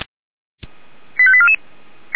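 Five-tone selective call (ZVEI), heard as quick runs of five tones, tones of about 70 ms. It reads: 85844
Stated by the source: ZVEI five-tone selective call received by SDR.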